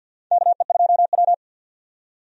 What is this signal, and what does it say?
Morse code, heard as a keyed tone, about 51 wpm, 700 Hz. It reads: KE2W
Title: USA Call Signs - 4 Characters - 50wpm